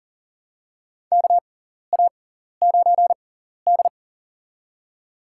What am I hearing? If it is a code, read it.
Morse code, heard as a keyed tone, about 37 wpm, spaced as for 14 wpm, 700 Hz. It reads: KA9D
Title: USA Call Signs - 4 Characters - 6x Character Spacing - 40wpm